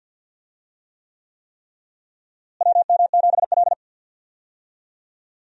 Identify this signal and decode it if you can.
Morse code, heard as a keyed tone, about 52 wpm, 700 Hz. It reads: WM7L
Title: USA Call Signs - 4 Characters - Repeated 3x - 50wpm